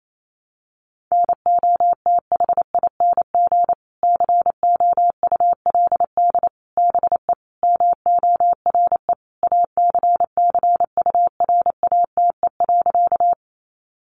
Morse code, heard as a keyed tone, about 28 wpm, 700 Hz. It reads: NOTHING COULD BE MORE ACCURATE.
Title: Sentences from Top 1200 Words 28wpm